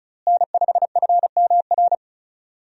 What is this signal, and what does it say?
Morse code, heard as a keyed tone, about 34 wpm, 700 Hz. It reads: N5FMR
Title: USA Call Signs - 5 Characters - Repeated 3x - 35wpm